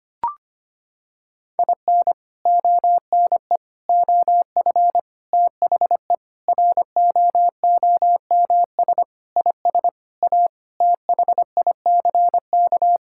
Morse code, heard as a keyed tone, about 25 wpm, 700 Hz. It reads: IN ONE OF THE ROOMS IS A THICK
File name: practice / 25wpm / Sentences with 4-Word Phrases - 25wpm